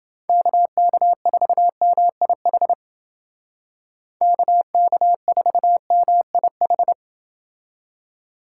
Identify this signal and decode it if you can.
Morse code, heard as a keyed tone, about 29 wpm, 700 Hz. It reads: KK4MIH KK4MIH